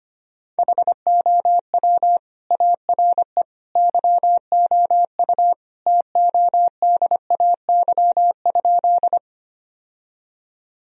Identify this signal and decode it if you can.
Morse code, heard as a keyed tone, about 25 wpm, 700 Hz. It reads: HOW ARE YOU TODAY?
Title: Sentences from Top 1000 Words - Review Part 4 - 25wpm